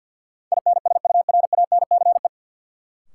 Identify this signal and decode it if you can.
Morse code, heard as a keyed tone, about 53 wpm, 700 Hz. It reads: INSURANCE
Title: Sentences from Top 1100 Words - New Word Drill Alt - 50wpm